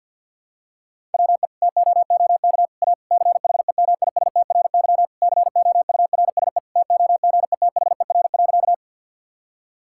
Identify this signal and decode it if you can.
Morse code, heard as a keyed tone, about 53 wpm, 700 Hz. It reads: WE TOOK A CHEMISTRY COURSE TOGETHER.
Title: Sentences from Top 1000 Words - Review Part 6 - 50wpm